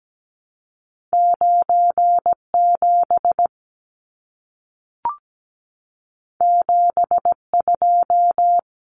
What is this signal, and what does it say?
Morse code, heard as a keyed tone, about 17 wpm, 700 Hz. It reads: 97 72
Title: Numbers - 2-Digit 17wpm